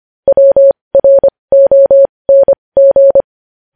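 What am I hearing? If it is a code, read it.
Morse code, heard as a keyed tone, about 25 wpm, spaced as for 20 wpm, 550 Hz. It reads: WRONG
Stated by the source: Morse code news headlines practice at 20 wpm.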